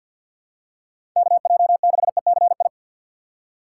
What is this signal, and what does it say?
Morse code, heard as a keyed tone, about 49 wpm, 700 Hz. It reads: KJ6ECI
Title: USA Call Signs - 6 Characters - 8x Character Spacing - 50wpm